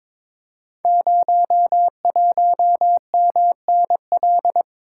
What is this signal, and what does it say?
Morse code, heard as a keyed tone, about 22 wpm, 700 Hz. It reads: 01MNL
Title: Code Groups - 5 Letter:Numbers - 22wpm